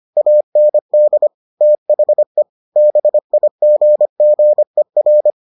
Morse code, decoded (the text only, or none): AND THE BIGGER